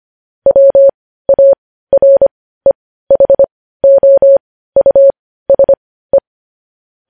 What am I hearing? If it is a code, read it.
Morse code, heard as a keyed tone, about 25 wpm, spaced as for 15 wpm, 550 Hz. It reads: WAREHOUSE
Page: news headlines in morse code at 15 wpm